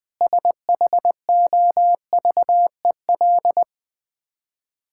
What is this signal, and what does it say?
Morse code, heard as a keyed tone, about 21 wpm, 700 Hz. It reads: SHOVEL
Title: Easy Words - Speed-Racing with Hints - 20wpm